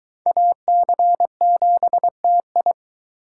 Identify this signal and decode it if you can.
Morse code, heard as a keyed tone, about 24 wpm, 700 Hz. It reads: AC7TI